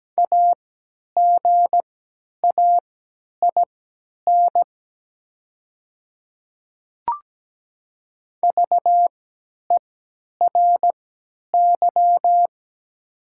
Morse code, heard as a keyed tone, about 17 wpm, spaced as for 10 wpm, 700 Hz. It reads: AGAIN VERY
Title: 3x Character Spacing - Top 300 Words - 17wpm